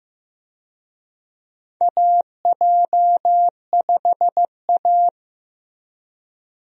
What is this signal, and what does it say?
Morse code, heard as a keyed tone, about 15 wpm, 700 Hz. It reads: AJ5A
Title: USA Call Signs - 4 Characters - Repeated 2x - 15wpm